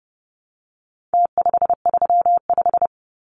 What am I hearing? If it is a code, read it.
Morse code, heard as a keyed tone, about 30 wpm, 700 Hz. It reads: T535
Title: Numbers - Contest Style - 4-Digit 30wpm